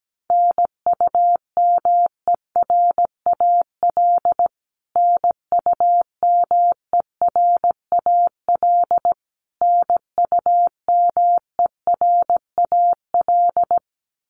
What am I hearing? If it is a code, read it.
Morse code, heard as a keyed tone, about 17 wpm, 700 Hz. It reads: NUMERAL NUMERAL NUMERAL
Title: Top 500 Words Repeated 17wpm